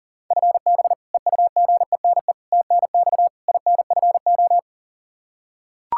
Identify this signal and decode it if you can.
Morse code, heard as a keyed tone, about 37 wpm, 700 Hz. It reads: FB EUGENE TNX INFO
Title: Practice QSO - Exchange 3 - More Info - 40wpm